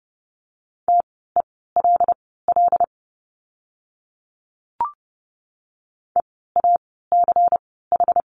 Morse code, heard as a keyed tone, about 30 wpm, spaced as for 17 wpm, 700 Hz. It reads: TELL EACH
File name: Lesson 16f - Letter c - Words 30wpm